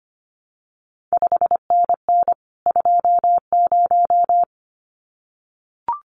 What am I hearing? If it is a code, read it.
Morse code, heard as a keyed tone, about 25 wpm, 700 Hz. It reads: 5NN 20
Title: CQ WW DX Contest Exchange - 25wpm